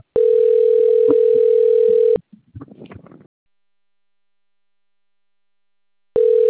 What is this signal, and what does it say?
Ringback tone (North America), 440 + 480 Hz.